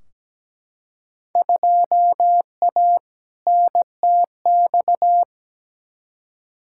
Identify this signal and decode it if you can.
Morse code, heard as a keyed tone, about 17 wpm, 700 Hz. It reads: 2A NTX
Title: Field Day Exchanges - 17wpm